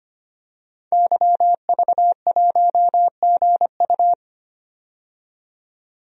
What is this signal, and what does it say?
Morse code, heard as a keyed tone, about 25 wpm, 700 Hz. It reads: YV1GU
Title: DXCC-100 Call Signs - 5 Characters - 25wpm